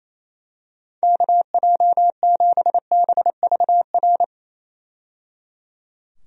Morse code, heard as a keyed tone, about 28 wpm, 700 Hz. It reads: KJ7BVR
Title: USA Call Signs - 6 Characters - 28wpm